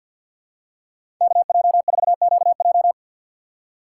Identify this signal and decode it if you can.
Morse code, heard as a keyed tone, about 51 wpm, 700 Hz. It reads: KJ4QJ
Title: USA Call Signs - 5 Characters - 4x Character Spacing - 50wpm